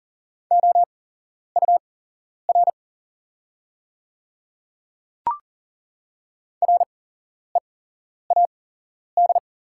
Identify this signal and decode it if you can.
Morse code, heard as a keyed tone, about 40 wpm, spaced as for 11 wpm, 700 Hz. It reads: OUR READ
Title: 8x Character Spacing - Top 300 Words - 40wpm